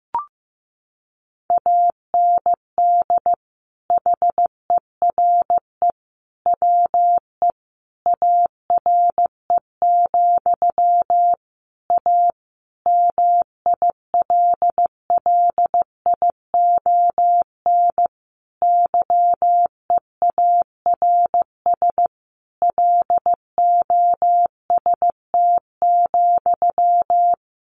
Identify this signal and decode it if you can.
Morse code, heard as a keyed tone, about 15 wpm, 700 Hz. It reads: AND HERE WE ARE, A MILLION YEARS LOST,
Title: A Sound Of Thunder 15wpm